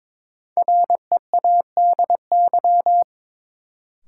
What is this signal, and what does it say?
Morse code, heard as a keyed tone, about 23 wpm, 700 Hz. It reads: READY
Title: Sentences from Top 400 Words - New Word Drill - 22wpm